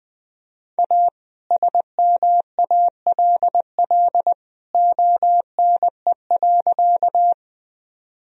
Morse code, heard as a keyed tone, about 20 wpm, 700 Hz. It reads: A SMALL ONE.
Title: Sentences from Top 1000 Words - Review Part 1 - 20wpm